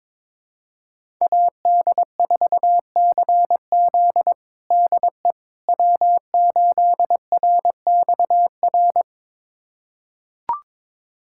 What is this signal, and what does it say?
Morse code, heard as a keyed tone, about 22 wpm, 700 Hz. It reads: AD4CZ DE W8RXR
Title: USA Callsign DE Callsign - 5 Characters - 22wpm